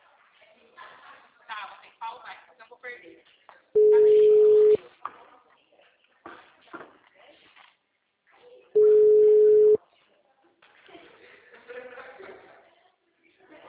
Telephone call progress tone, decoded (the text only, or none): ringback tone